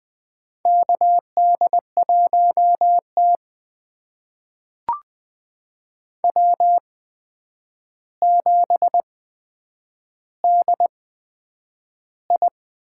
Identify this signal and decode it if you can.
Morse code, heard as a keyed tone, about 20 wpm, spaced as for 9 wpm, 700 Hz. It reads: KD1T W7DI
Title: USA Call Signs - 4 Characters - 8x Character Spacing - 20wpm